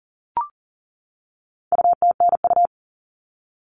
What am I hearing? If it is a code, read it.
Morse code, heard as a keyed tone, about 40 wpm, 700 Hz. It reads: UTNU
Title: All Cut Numbers - 4-Digit 40wpm